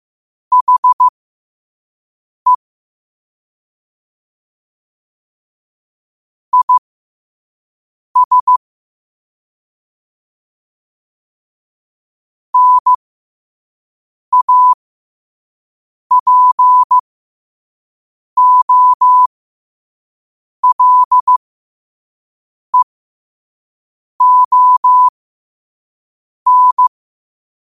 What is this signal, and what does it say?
Morse code, heard as a keyed tone, about 15 wpm, spaced as for 5 wpm, 1000 Hz. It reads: HE IS NAPOLEON
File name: qotdfiles / 5wpm